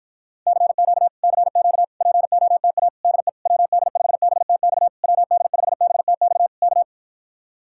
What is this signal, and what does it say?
Morse code, heard as a keyed tone, about 53 wpm, 700 Hz. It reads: CQ CQ POTA DE WD5BTX WD5BTX K